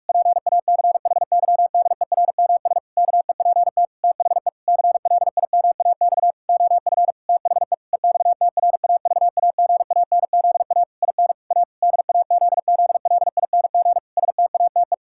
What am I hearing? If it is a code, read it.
Morse code, heard as a keyed tone, about 46 wpm, 700 Hz. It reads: PACHYDERMS KEPT THE CLIMAX OF THE EXTRAVAGANZA IN A DAZZLING STATE
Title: Pangram Challenge v2 - 45wpm